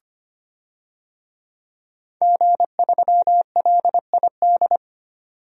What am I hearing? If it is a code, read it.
Morse code, heard as a keyed tone, about 25 wpm, 700 Hz. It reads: G3LID
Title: DXCC-100 Call Signs - 5 Characters - 25wpm